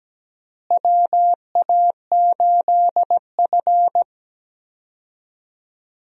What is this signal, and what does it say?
Morse code, heard as a keyed tone, about 17 wpm, 700 Hz. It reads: WA8F
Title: USA Call Signs - 4 Characters - Repeated 3x - 17wpm